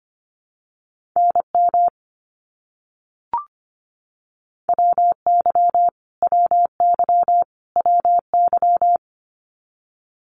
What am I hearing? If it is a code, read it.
Morse code, heard as a keyed tone, about 25 wpm, 700 Hz. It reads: NM WY WY WY
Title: US States Repeated 3x 25wpm